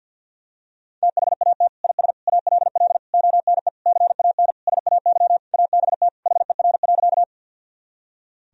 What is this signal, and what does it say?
Morse code, heard as a keyed tone, about 50 wpm, 700 Hz. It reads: THAT IS ALL ONE CAN SAY ABT HER.
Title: Sentences from Top 200 Words and Abbreviations 50wpm